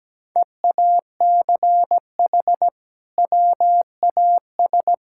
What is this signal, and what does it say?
Morse code, heard as a keyed tone, about 17 wpm, 700 Hz. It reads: EACH WAS
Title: Sentences from Top 1600 Words 17wpm